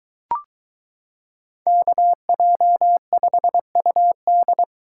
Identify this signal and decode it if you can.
Morse code, heard as a keyed tone, about 23 wpm, 700 Hz. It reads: KJ5UD